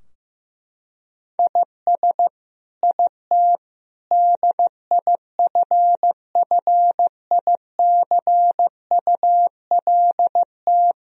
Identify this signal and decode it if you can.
Morse code, heard as a keyed tone, about 15 wpm, 700 Hz. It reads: IS IT DIFFICULT
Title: Sentences from Top 1000 Words - Review Part 4 - 15wpm